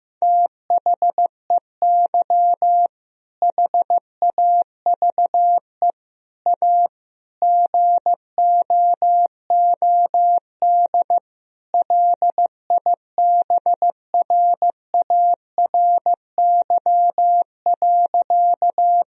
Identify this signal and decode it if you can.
Morse code, heard as a keyed tone, about 16 wpm, 700 Hz. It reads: THEY HAVE A GOOD LIBRARY.
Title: Sentences from Top 1000 Words - Review Part 4 - 15wpm